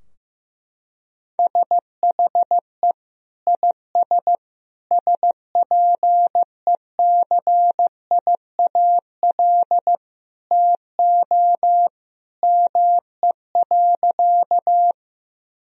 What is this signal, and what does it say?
Morse code, heard as a keyed tone, about 16 wpm, 700 Hz. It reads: SHE IS SPECIAL TO ME.
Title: Sentences from Top 1000 Words - Review Part 3 - 15wpm